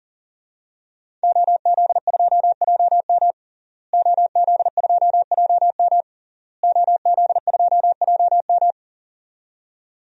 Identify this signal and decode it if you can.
Morse code, heard as a keyed tone, about 40 wpm, 700 Hz. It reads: OZ2JM OZ2JM OZ2JM
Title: DXCC-100 Call Signs - 5 Characters - Repeated 3x - 40wpm